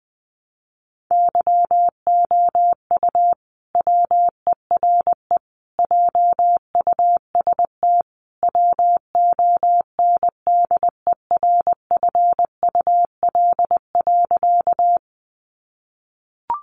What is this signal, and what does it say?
Morse code, heard as a keyed tone, about 20 wpm, 700 Hz. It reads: YOU WERE JUST WONDERFUL.
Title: Sentences from Top 1100 Words 20wpm